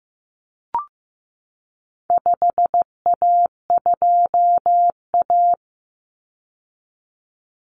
Morse code, heard as a keyed tone, about 15 wpm, 700 Hz. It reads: 5A2A
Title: Common Cut Numbers - 4-Digit 15wpm